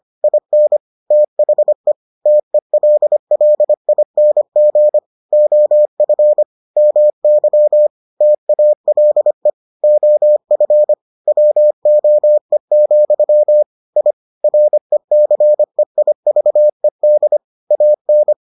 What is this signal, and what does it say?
Morse code, heard as a keyed tone, about 25 wpm, 600 Hz. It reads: IN THE TELLING OF MY TALE OF WOE, I RECEIVED AN